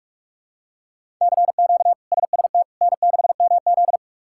Morse code, heard as a keyed tone, about 44 wpm, 700 Hz. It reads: CQ SST N6MZ
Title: K1USN SST QSO Practice - 45wpm